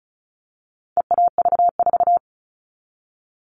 Morse code, heard as a keyed tone, about 34 wpm, 700 Hz. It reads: EAV4
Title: All Cut Numbers - 4-Digit 35wpm